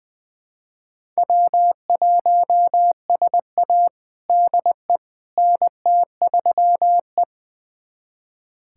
Morse code, heard as a keyed tone, about 21 wpm, 700 Hz. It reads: W1SA DE NT3E
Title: USA Callsign DE Callsign - 4 Characters - 20wpm